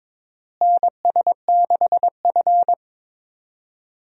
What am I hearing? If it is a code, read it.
Morse code, heard as a keyed tone, about 21 wpm, 700 Hz. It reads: NS6F
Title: USA Call Signs - 4 Characters - Repeated 3x - 22wpm